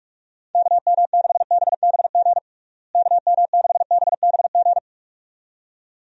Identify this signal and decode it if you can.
Morse code, heard as a keyed tone, about 41 wpm, 700 Hz. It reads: KM6BBG KM6BBG
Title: USA Call Signs - 6 Characters - Repeated 2x - 45wpm